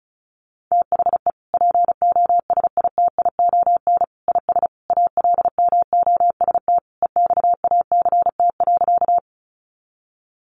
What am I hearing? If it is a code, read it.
Morse code, heard as a keyed tone, about 34 wpm, 700 Hz. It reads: THE POSITION IS ALMOST EXACT.
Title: Sentences from Top 1300 Words 35wpm